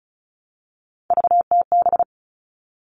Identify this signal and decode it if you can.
Morse code, heard as a keyed tone, about 34 wpm, 700 Hz. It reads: VTB
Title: All Cut Numbers - 3-Digit 35wpm